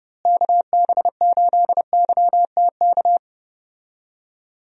Morse code, heard as a keyed tone, about 30 wpm, 700 Hz. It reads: KB8YTK
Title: USA Call Signs - 6 Characters - Repeated 2x - 30wpm